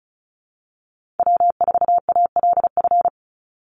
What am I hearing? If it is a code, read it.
Morse code, heard as a keyed tone, about 34 wpm, 700 Hz. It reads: W4ALF
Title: Call Signs - 35wpm